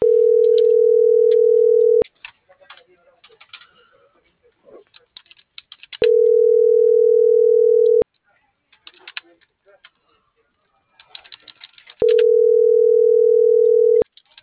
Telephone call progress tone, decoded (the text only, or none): ringback tone